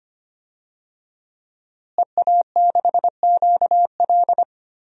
Morse code, heard as a keyed tone, about 25 wpm, 700 Hz. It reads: EA6QL
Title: DXCC-100 Call Signs - 5 Characters - Repeated 2x - 25wpm